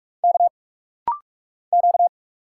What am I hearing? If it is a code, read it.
Morse code, heard as a keyed tone, about 46 wpm, 700 Hz. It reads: K Q
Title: Single Letter-Number Alt - Rapid-Fire 45wpm